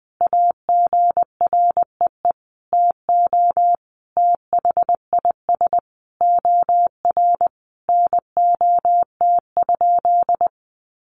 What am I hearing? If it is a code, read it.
Morse code, heard as a keyed tone, about 19 wpm, 700 Hz. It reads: AGREE TO THIS OR NOT?
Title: Sentences from Top 1100 Words 20wpm